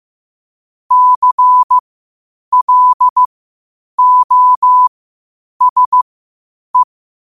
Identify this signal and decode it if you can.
Morse code, heard as a keyed tone, about 15 wpm, spaced as for 8 wpm, 1000 Hz. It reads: CLOSE
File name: qotdfiles / 8wpm